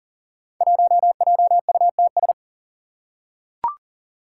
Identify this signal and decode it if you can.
Morse code, heard as a keyed tone, about 38 wpm, 700 Hz. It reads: 1JUTS
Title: Five Challenging Letter-Number Combinations 40wpm